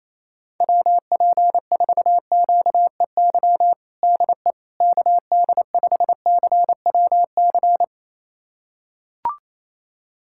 Morse code, heard as a keyed tone, about 28 wpm, 700 Hz. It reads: WP4QEY DE KD5CWC
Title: USA Callsign DE Callsign 28wpm